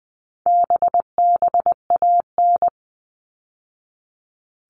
Morse code, heard as a keyed tone, about 20 wpm, 700 Hz. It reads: BBAN
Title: All Cut Numbers - 4-Digit 20wpm